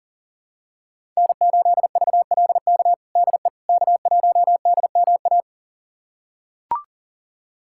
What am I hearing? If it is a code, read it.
Morse code, heard as a keyed tone, about 37 wpm, 700 Hz. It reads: N8VLK DE K1DMA
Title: USA Callsign DE Callsign - 5 Characters - 40wpm